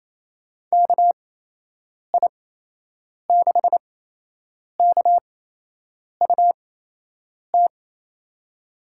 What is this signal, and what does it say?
Morse code, heard as a keyed tone, about 28 wpm, spaced as for 8 wpm, 700 Hz. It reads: KI6KUT
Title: USA Call Signs - 8x Character Spacing - 28wpm